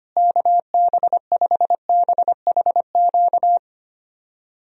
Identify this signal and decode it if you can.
Morse code, heard as a keyed tone, about 25 wpm, 700 Hz. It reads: KB5BHQ
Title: USA Call Signs - 6 Characters - Repeated 3x - 25wpm